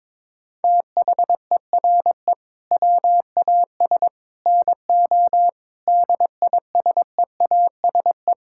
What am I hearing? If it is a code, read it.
Morse code, heard as a keyed tone, about 22 wpm, 700 Hz. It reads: THERE WAS NO DISEASE
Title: Sentences from Top 1000 Words - Review Part 6 - 22wpm